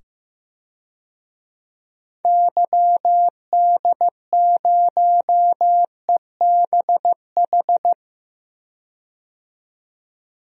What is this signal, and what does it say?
Morse code, heard as a keyed tone, about 15 wpm, 700 Hz. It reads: YD0EBH
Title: DXCC-100 Call Signs - 6 Characters - 15wpm